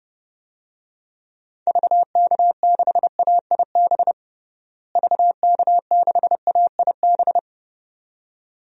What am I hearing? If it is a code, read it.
Morse code, heard as a keyed tone, about 30 wpm, 700 Hz. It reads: VK6AIB VK6AIB
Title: DXCC-100 Call Signs - 6 Characters - Repeated 2x - 30wpm